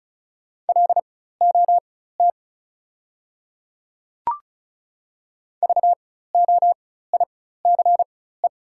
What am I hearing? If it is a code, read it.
Morse code, heard as a keyed tone, about 34 wpm, spaced as for 16 wpm, 690 Hz. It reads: LOT VOICE
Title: 4x Character Spacing - Top 500 Words - 35wpm